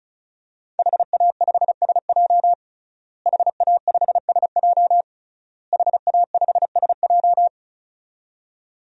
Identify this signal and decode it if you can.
Morse code, heard as a keyed tone, about 34 wpm, 700 Hz. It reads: HA5SJ HA5SJ HA5SJ
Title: DXCC-100 Call Signs - 5 Characters - Repeated 3x - 35wpm